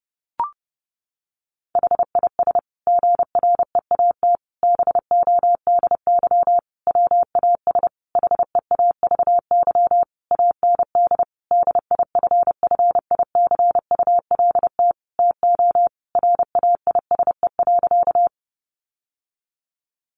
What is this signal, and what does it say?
Morse code, heard as a keyed tone, about 30 wpm, 700 Hz. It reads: HIS GREAT BODY WAS HEAVY AND DIFFICULT TO RAISE.
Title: Sentences from Top 1400 Words 30wpm